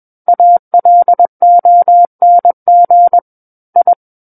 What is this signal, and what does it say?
Morse code, heard as a keyed tone, about 21 wpm, 700 Hz. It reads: ALONG I
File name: Book One / 12-Chapter 12 - 21 WPM